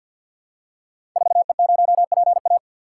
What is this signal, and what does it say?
Morse code, heard as a keyed tone, about 50 wpm, 700 Hz. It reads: 4E0PA
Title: Code Groups - 5 Letter:Numbers - 50wpm